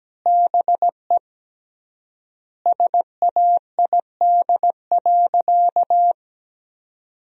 Morse code, heard as a keyed tone, about 17 wpm, 700 Hz. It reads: BE SAID.
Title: 3x Word-Spacing - Sentences from Top 100 Words - 17wpm